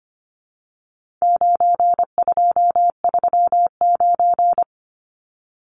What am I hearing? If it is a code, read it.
Morse code, heard as a keyed tone, about 25 wpm, 700 Hz. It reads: 9239
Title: Numbers - 4-Digit 25wpm